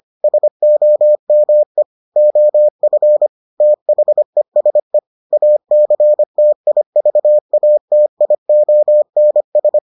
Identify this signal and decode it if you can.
Morse code, heard as a keyed tone, about 25 wpm, 600 Hz. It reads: SOME OF THESE ACTIVATIONS